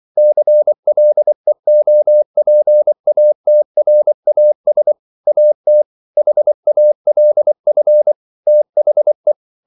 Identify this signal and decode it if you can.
Morse code, heard as a keyed tone, about 24 wpm, 600 Hz. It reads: CLEOPATRAS AT HALF THE